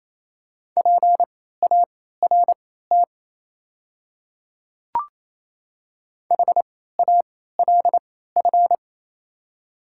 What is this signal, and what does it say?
Morse code, heard as a keyed tone, about 28 wpm, spaced as for 16 wpm, 700 Hz. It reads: PART HALF
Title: 3x Character Spacing - Top 500 Words - 28wpm